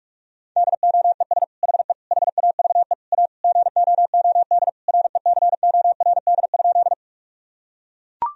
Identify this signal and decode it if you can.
Morse code, heard as a keyed tone, about 41 wpm, 700 Hz. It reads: DOES HE HAVE A GOOD RECORD?
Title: Sentences from Top 1000 Words - Review Part 3 - 45wpm